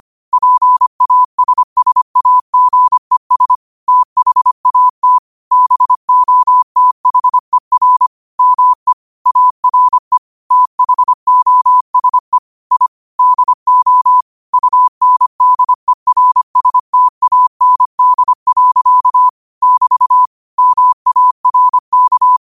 Morse code, heard as a keyed tone, about 25 wpm, 1000 Hz. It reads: PASSAGES THAT BOTHER ME ARE THOSE I DO UNDERSTAND. <BT> MARK